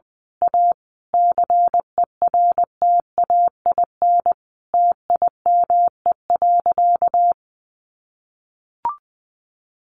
Morse code, heard as a keyed tone, about 20 wpm, 700 Hz. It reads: A CERTAIN TIME.
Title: Sentences from Top 1000 Words 20wpm